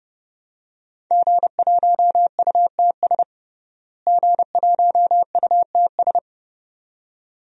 Morse code, heard as a keyed tone, about 30 wpm, 700 Hz. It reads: G1UTS G1UTS
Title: DXCC-100 Call Signs - 5 Characters - Repeated 2x - 30wpm